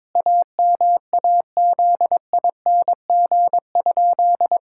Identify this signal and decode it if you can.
Morse code, heard as a keyed tone, about 22 wpm, 700 Hz. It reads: AMAZING?
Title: Pangram Challenge v2 - 22wpm